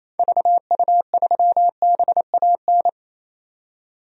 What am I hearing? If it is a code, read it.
Morse code, heard as a keyed tone, about 27 wpm, 700 Hz. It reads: VU3BAN
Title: DXCC-100 Call Signs - 6 Characters - 28wpm